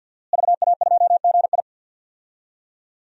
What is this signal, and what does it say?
Morse code, heard as a keyed tone, about 49 wpm, 700 Hz. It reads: VA2GI
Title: DXCC-100 Call Signs - 5 Characters - Repeated 2x - 50wpm